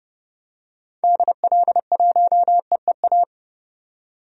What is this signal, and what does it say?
Morse code, heard as a keyed tone, about 30 wpm, 700 Hz. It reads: DL1EEA
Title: DXCC-100 Call Signs - 6 Characters - 30wpm